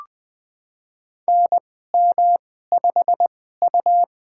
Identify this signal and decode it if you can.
Morse code, heard as a keyed tone, about 20 wpm, spaced as for 15 wpm, 700 Hz. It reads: NM5U
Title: USA Call Signs - 4 Characters - 2x Character Spacing - 20wpm